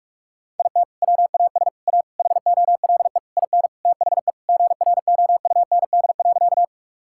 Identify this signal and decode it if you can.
Morse code, heard as a keyed tone, about 41 wpm, 700 Hz. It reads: IT WAS A HOLE IN THE GROUND.
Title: Sentences from Top 1600 Words 45wpm